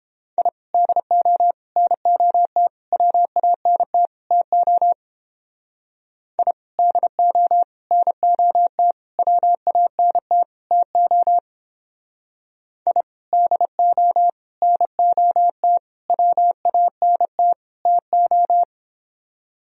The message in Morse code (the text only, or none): I DO NOT WANT TO I DO NOT WANT TO I DO NOT WANT TO